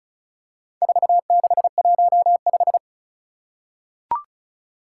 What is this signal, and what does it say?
Morse code, heard as a keyed tone, about 36 wpm, 700 Hz. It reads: 4615
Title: Four Challenging Letter-Number Combinations 35wpm